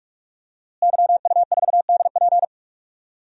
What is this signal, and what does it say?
Morse code, heard as a keyed tone, about 46 wpm, 700 Hz. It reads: YU4DP
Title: DXCC-100 Call Signs - 5 Characters - Repeated 2x - 45wpm